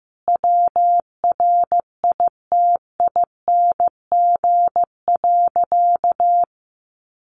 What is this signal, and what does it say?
Morse code, heard as a keyed tone, about 15 wpm, 700 Hz. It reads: WRITING.
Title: Sentences from Top 800 Words 15wpm